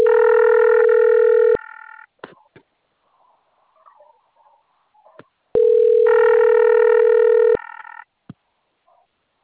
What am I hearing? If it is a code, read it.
Ringback tone (North America), 440 + 480 Hz.